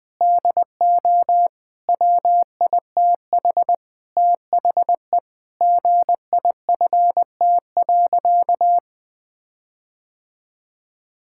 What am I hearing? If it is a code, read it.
Morse code, heard as a keyed tone, about 20 wpm, 700 Hz. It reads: DO WITH THE GIFT.
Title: Sentences with 5-Word Phrases - 20wpm